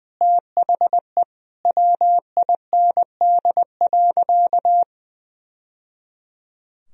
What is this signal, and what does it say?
Morse code, heard as a keyed tone, about 20 wpm, 700 Hz. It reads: THE WIND.
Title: Sentences from Top 1000 Words - Review Part 2 - 20wpm